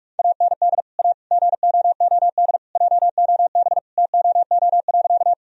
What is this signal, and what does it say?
Morse code, heard as a keyed tone, about 46 wpm, 700 Hz. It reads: AND A GOOD JOB TOO.
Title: Sentences from Top 1000 Words - Review Part 3 - 45wpm